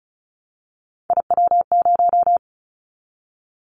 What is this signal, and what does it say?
Morse code, heard as a keyed tone, about 35 wpm, 700 Hz. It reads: IW0
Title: Callsign Prefixes for Top 50 DXCC - 35wpm